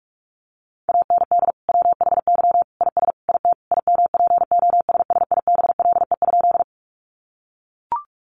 Morse code, heard as a keyed tone, about 44 wpm, 700 Hz. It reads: AND WHY IS IT IMPOSSIBLE?
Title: Sentences from Top 1100 Words 45wpm